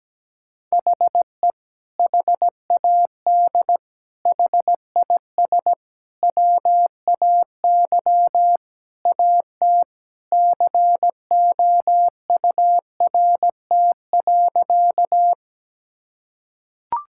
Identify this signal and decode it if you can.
Morse code, heard as a keyed tone, about 17 wpm, 700 Hz. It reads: HE HAD HIS WAY AT COURT.
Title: Sentences from Top 1600 Words 17wpm